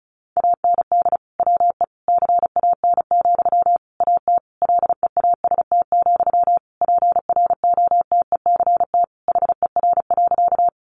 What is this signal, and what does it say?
Morse code, heard as a keyed tone, about 34 wpm, 700 Hz. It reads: AND WE CAN, AT LEAST, PROTECT HER.
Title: Sentences from Top 900 Words 35wpm